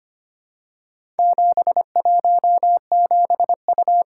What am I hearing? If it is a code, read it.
Morse code, heard as a keyed tone, about 25 wpm, 700 Hz. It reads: 717U
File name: practice / 25wpm / Four Letter-Number Combinations - 25wpm